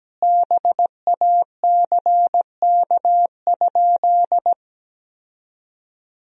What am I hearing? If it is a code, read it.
Morse code, heard as a keyed tone, about 17 wpm, 700 Hz. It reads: BACK?
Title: 3x Word-Spacing - Sentences from Top 400 Words - 17wpm